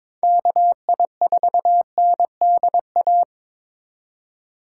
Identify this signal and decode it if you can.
Morse code, heard as a keyed tone, about 22 wpm, 700 Hz. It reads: KI4NDA
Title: USA Call Signs - 6 Characters - 3x Character Spacing - 22wpm